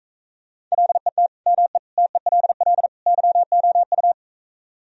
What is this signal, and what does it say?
Morse code, heard as a keyed tone, about 45 wpm, 700 Hz. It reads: LET ME TELL YOU